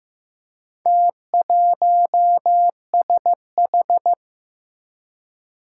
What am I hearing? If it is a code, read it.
Morse code, heard as a keyed tone, about 15 wpm, 700 Hz. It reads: T1SH
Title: Four Challenging Letter-Number Combinations 15wpm